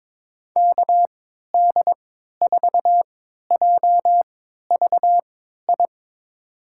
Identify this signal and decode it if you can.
Morse code, heard as a keyed tone, about 22 wpm, spaced as for 13 wpm, 700 Hz. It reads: KD4JVI
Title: USA Call Signs - 6 Characters - 3x Character Spacing - 22wpm